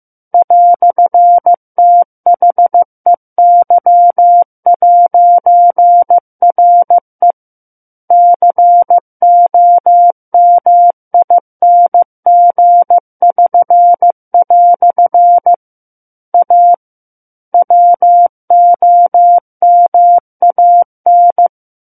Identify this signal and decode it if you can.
Morse code, heard as a keyed tone, about 15 wpm, 700 Hz. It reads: "THEY'RE COMING<SN>" A WOMAN
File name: Book One / 06-Chapter 6 - 15 WPM